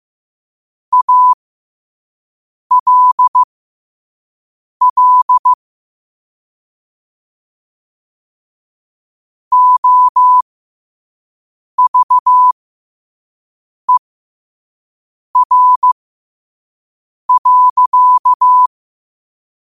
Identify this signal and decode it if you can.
Morse code, heard as a keyed tone, about 15 wpm, spaced as for 5 wpm, 1000 Hz. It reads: ALL OVER.
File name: qotdfiles / 5wpm